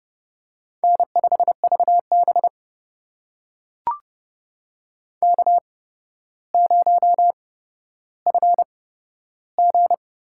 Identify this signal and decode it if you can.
Morse code, heard as a keyed tone, about 30 wpm, spaced as for 13 wpm, 700 Hz. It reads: N5VB K0FG
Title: USA Call Signs - 4 Characters - 8x Character Spacing - 30wpm